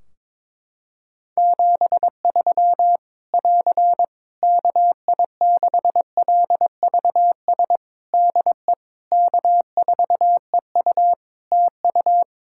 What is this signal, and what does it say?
Morse code, heard as a keyed tone, about 22 wpm, 700 Hz. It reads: 73 <AR> KI6LVS DE K4EU TU
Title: K1USN SST QSO Practice 22wpm